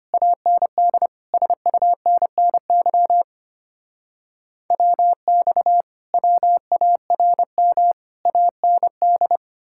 Morse code, heard as a keyed tone, about 33 wpm, 700 Hz. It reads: AND SUNNY WX WARM AND